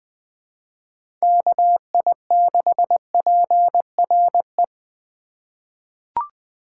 Morse code, heard as a keyed tone, about 20 wpm, 700 Hz. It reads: KI6PRE
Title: USA Call Signs - 2x Character Spacing - 20wpm